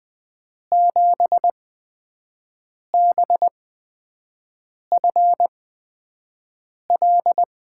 Morse code, heard as a keyed tone, about 20 wpm, spaced as for 5 wpm, 700 Hz. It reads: 7BFL
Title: USA Call Signs - 8x Character Spacing - 20wpm